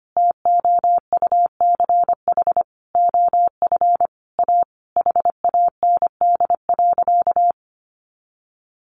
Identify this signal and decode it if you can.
Morse code, heard as a keyed tone, about 25 wpm, 700 Hz. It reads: TOUCH OF A HAND.
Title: A Sound Of Thunder 25wpm